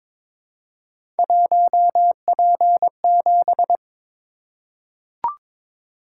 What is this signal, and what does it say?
Morse code, heard as a keyed tone, about 22 wpm, 700 Hz. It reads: 1P7
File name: practice / 22wpm / Three Letter-Number Combinations - 22wpm